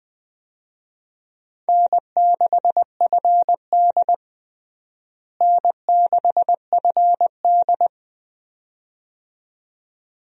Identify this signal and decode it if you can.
Morse code, heard as a keyed tone, about 20 wpm, 700 Hz. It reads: N6FD N6FD
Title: USA Call Signs - 4 Characters - Repeated 2x - 20wpm